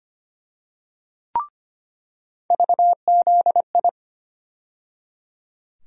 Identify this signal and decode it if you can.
Morse code, heard as a keyed tone, about 25 wpm, 700 Hz. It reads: VZI
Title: USA Call Sign Postfixes - 3 Letters 25wpm